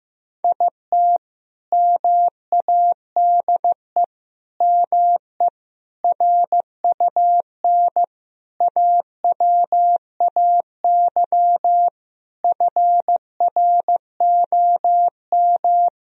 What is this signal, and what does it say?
Morse code, heard as a keyed tone, about 15 wpm, 700 Hz. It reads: IT MADE ME RUN AWAY FROM
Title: Sentences from Top 1000 Words - Review Part 1 - 15wpm